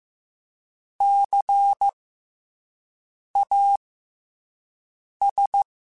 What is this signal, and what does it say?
Morse code, heard as a keyed tone, about 15 wpm, spaced as for 5 wpm, 800 Hz. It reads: CAS
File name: Special INGs-mp3s / Special INGs-mp3s / SlowToFaster ing @10